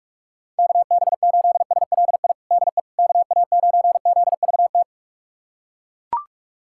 Morse code, heard as a keyed tone, about 41 wpm, 700 Hz. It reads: KB8SLI DE KA9ZVT
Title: USA Callsign DE Callsign - 6 Characters - 45wpm